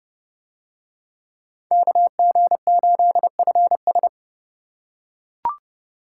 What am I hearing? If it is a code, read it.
Morse code, heard as a keyed tone, about 28 wpm, 700 Hz. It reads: KG8FS